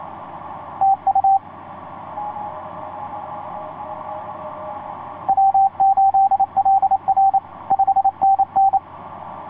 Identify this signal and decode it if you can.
Morse code, heard as a keyed tone, about 28 wpm, 780 Hz. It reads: TU W8LR 5NN
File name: 6w-gm4fdm f6bee etc 15m cw 30jan2010 1502